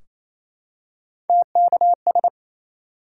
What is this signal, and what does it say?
Morse code, heard as a keyed tone, about 28 wpm, 700 Hz. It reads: TKS
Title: Top 90 QSO Elements 28wpm